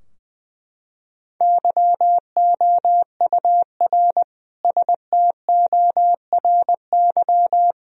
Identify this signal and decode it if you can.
Morse code, heard as a keyed tone, about 20 wpm, 700 Hz. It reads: YOUR STORY
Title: Sentences from Top 1000 Words - Review Part 2 - 20wpm